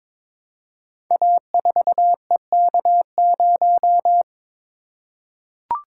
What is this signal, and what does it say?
Morse code, heard as a keyed tone, about 22 wpm, 700 Hz. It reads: A4EK0
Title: Code Groups - 5 Letter:Numbers - 22wpm